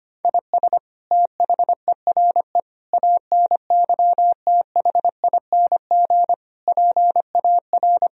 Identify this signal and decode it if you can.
Morse code, heard as a keyed tone, about 25 wpm, 700 Hz. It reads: IS THERE ANYTHING PAR
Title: Sentences from Top 1000 Words - Review Part 4 - 25wpm